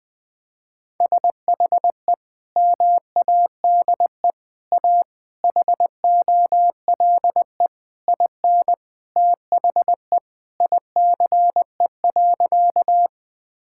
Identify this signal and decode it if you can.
Morse code, heard as a keyed tone, about 21 wpm, 700 Hz. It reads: SHE MADE A HOLE IN THE ICE.
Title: Sentences from Top 1600 Words 20wpm